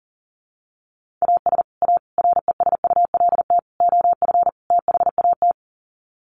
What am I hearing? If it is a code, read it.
Morse code, heard as a keyed tone, about 40 wpm, 700 Hz. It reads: AS A RESULT OF THAT
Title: Sets of 5 Words Encore 40wpm